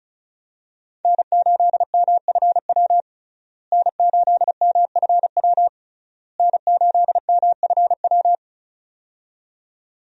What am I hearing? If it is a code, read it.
Morse code, heard as a keyed tone, about 34 wpm, 700 Hz. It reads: N8MFW N8MFW N8MFW
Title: USA Call Signs - 5 Characters - Repeated 3x - 35wpm